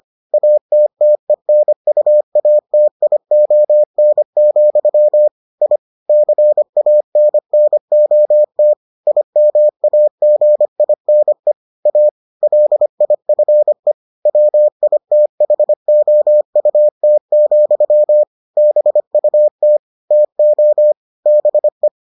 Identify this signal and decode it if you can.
Morse code, heard as a keyed tone, about 25 wpm, 600 Hz. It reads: ATTENUATION, I CANNOT IMAGINE A LIFE WITHOUT, BUT TO BE